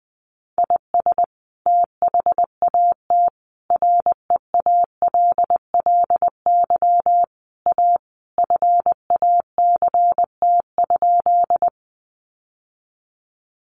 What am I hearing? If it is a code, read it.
Morse code, heard as a keyed tone, about 20 wpm, 700 Hz. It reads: IS THAT REALLY A FACT?
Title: Sentences from Top 600 Words 20wpm